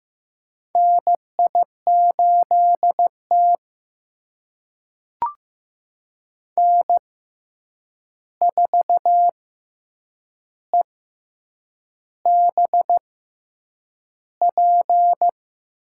Morse code, heard as a keyed tone, about 15 wpm, spaced as for 7 wpm, 700 Hz. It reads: NI8T N4EBP
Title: USA Call Signs - 6x Character Spacing - 15wpm